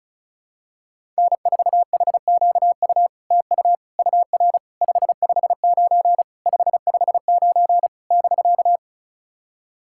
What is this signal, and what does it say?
Morse code, heard as a keyed tone, about 36 wpm, 700 Hz. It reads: N4HQU TU UR 559 559 <BK>
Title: SOTA QSO Practice 35wpm